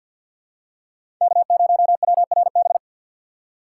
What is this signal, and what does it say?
Morse code, heard as a keyed tone, about 46 wpm, 700 Hz. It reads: K0WRB
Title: USA Call Signs - 5 Characters - 3x Character Spacing - 50wpm